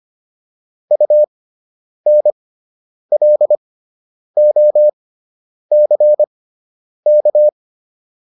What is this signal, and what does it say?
Morse code, heard as a keyed tone, about 25 wpm, spaced as for 9 wpm, 600 Hz. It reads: UNLOCK